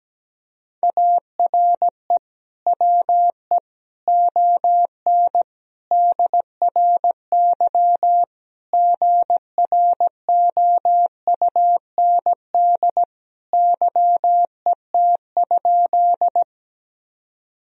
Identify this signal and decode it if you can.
Morse code, heard as a keyed tone, about 17 wpm, 700 Hz. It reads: ARE WE ON DRY GROUND YET?
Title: Sentences from Top 1000 Words - Review Part 3 - 17wpm